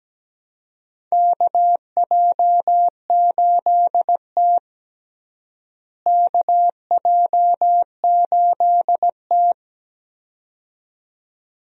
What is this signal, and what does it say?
Morse code, heard as a keyed tone, about 17 wpm, 700 Hz. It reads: KJ8T KJ8T
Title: USA Call Signs - 4 Characters - Repeated 2x - 17wpm